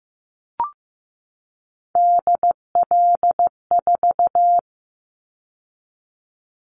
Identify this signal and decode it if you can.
Morse code, heard as a keyed tone, about 15 wpm, 700 Hz. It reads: DL4
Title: Callsign Prefixes for Top 50 DXCC - 15wpm